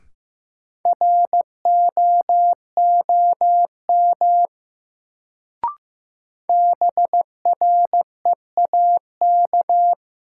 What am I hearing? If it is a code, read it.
Morse code, heard as a keyed tone, about 15 wpm, 700 Hz. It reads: ROOM BREAK
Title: Sets of 4 Words - New Word Drill - 15wpm